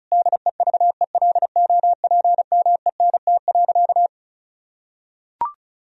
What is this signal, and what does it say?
Morse code, heard as a keyed tone, about 36 wpm, 700 Hz. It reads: DEVELOPMENT.
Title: Sentences from Top 1000 Words - Review Part 4 - 35wpm